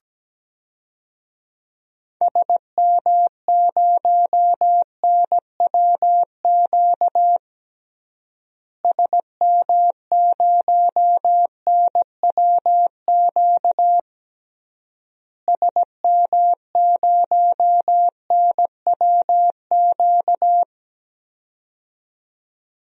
Morse code, heard as a keyed tone, about 17 wpm, 700 Hz. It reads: SM0NWQ SM0NWQ SM0NWQ
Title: DXCC-100 Call Signs - 6 Characters - Repeated 3x - 17wpm